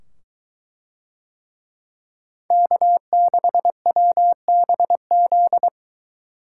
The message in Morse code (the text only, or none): K6WBZ